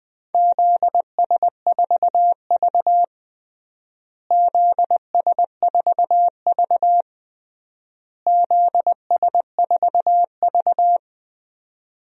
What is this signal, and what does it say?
Morse code, heard as a keyed tone, about 20 wpm, 700 Hz. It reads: ZS4V ZS4V ZS4V
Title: DXCC-100 Call Signs - 4 Characters - Repeated 3x - 20wpm